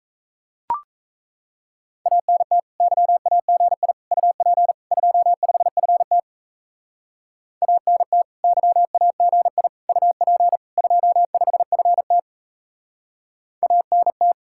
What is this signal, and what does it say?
Morse code, heard as a keyed tone, about 35 wpm, 700 Hz. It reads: ANT YAGI UP 25FT ANT YAGI UP 25FT ANT